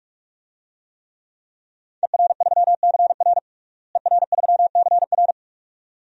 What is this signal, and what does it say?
Morse code, heard as a keyed tone, about 46 wpm, 700 Hz. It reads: ER3CR ER3CR
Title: DXCC-100 Call Signs - 5 Characters - Repeated 2x - 45wpm